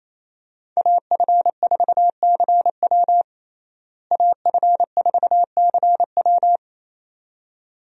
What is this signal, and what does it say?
Morse code, heard as a keyed tone, about 28 wpm, 700 Hz. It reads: AF4CW AF4CW
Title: USA Call Signs - 5 Characters - Repeated 2x - 28wpm